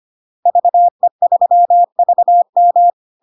Morse code, heard as a keyed tone, about 25 wpm, 700 Hz. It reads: VE3VM
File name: Log-in-CW-25wpm-700-Hz